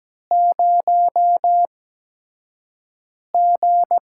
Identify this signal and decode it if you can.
Morse code, heard as a keyed tone, about 17 wpm, spaced as for 5 wpm, 700 Hz. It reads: 0G
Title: USA Call Signs - 8x Character Spacing - 17wpm